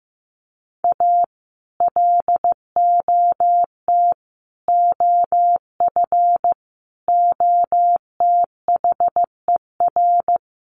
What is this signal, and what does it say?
Morse code, heard as a keyed tone, about 15 wpm, 700 Hz. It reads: A LOT OF OTHER
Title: Sets of 5 Words Encore2 15wpm